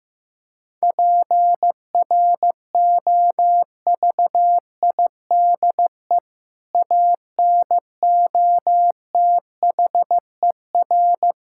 Morse code, heard as a keyed tone, about 16 wpm, 700 Hz. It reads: PROVIDE ANOTHER
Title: Sentences from Top 1000 Words - Review Part 5 - 15wpm